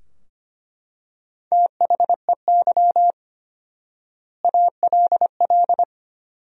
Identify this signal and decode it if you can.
Morse code, heard as a keyed tone, about 25 wpm, 700 Hz. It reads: THEY ALL
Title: 4x Word-Spacing - Sentences from Top 400 Words - 25wpm